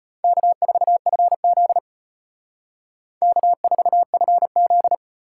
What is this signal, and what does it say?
Morse code, heard as a keyed tone, about 32 wpm, 700 Hz. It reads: K4FZ K4FZ